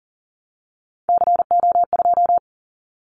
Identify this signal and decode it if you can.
Morse code, heard as a keyed tone, about 40 wpm, 700 Hz. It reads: CO2
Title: Callsign Prefixes for Top 100 DXCC - 40wpm